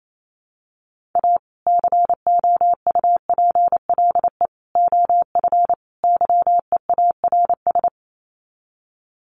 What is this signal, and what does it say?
Morse code, heard as a keyed tone, about 28 wpm, 700 Hz. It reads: A COUPLE OF YEARS
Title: Sets of 4 Words 28wpm